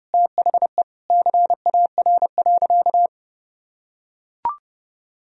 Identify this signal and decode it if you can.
Morse code, heard as a keyed tone, about 30 wpm, 700 Hz. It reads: THE CAR.
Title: Sentences from Top 1000 Words - Review Part 1 - 30wpm